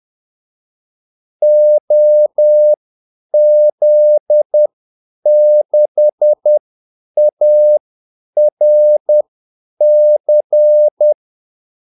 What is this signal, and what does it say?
Morse code, heard as a keyed tone, about 10 wpm, spaced as for 8 wpm, 600 Hz. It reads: OZ6ARC
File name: Callsigns-10-8wpm